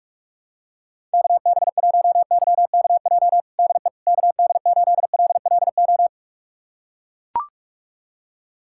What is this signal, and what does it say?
Morse code, heard as a keyed tone, about 42 wpm, 700 Hz. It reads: KB1YKJ DE KD8LLO
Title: USA Callsign DE Callsign - 6 Characters - 45wpm